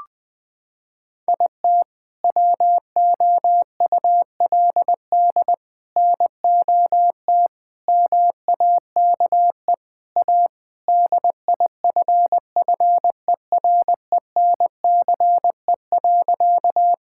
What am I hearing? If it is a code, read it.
Morse code, heard as a keyed tone, about 20 wpm, 700 Hz. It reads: IT WOULD NOT MAKE A DIFFERENCE.
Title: Sentences from Top 1000 Words - Review Part 5 - 20wpm